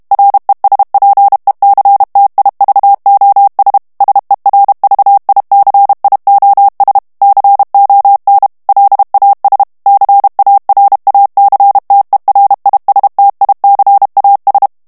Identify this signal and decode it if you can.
Morse code, heard as a keyed tone, about 32 wpm, 800 Hz. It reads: RESPECTIVOS SERVICIOS CON LAS CARACTERISTICAS
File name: ESPAGNOL TEXTE EN CLAIR 2 VITESSE 30 MOTS MINUTES